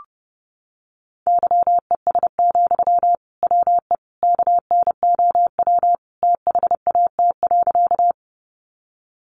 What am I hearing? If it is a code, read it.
Morse code, heard as a keyed tone, about 30 wpm, 700 Hz. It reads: YES, WE KNOW THAT.
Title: Sentences from Top 500 Words 30wpm